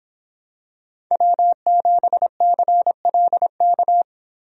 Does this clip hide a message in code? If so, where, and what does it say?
Yes: Morse code from 1.1 to 4.0 s, heard as a keyed tone, about 26 wpm, 700 Hz. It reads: W7CLK